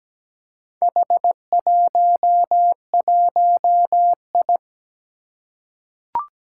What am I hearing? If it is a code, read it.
Morse code, heard as a keyed tone, about 17 wpm, 700 Hz. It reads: H11I
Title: Four Challenging Letter-Number Combinations 17wpm